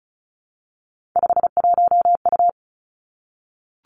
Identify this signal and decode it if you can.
Morse code, heard as a keyed tone, about 36 wpm, 700 Hz. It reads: 51U
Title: Three Challenging Letter-Number Combinations 35wpm